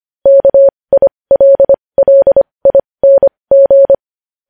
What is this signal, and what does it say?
Morse code, heard as a keyed tone, about 25 wpm, spaced as for 20 wpm, 550 Hz. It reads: KILLING
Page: news-headlines-in-morse-code-at-20-wpm